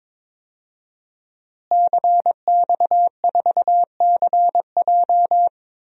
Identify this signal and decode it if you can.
Morse code, heard as a keyed tone, about 22 wpm, 700 Hz. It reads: CX4CJ
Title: DXCC-100 Call Signs - 5 Characters - Repeated 3x - 22wpm